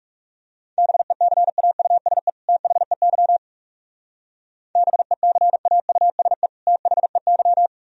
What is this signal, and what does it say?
Morse code, heard as a keyed tone, about 38 wpm, 700 Hz. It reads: BECAUSE THEY BECAUSE THEY